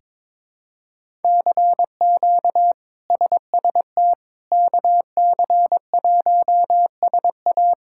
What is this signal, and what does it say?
Morse code, heard as a keyed tone, about 23 wpm, 700 Hz. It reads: CQ SST KC1SA
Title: K1USN SST QSO Practice - 22wpm